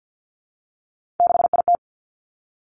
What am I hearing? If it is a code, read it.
Morse code, heard as a keyed tone, about 50 wpm, 700 Hz. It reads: 6IT
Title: Three Challenging Letter-Number Combinations 50wpm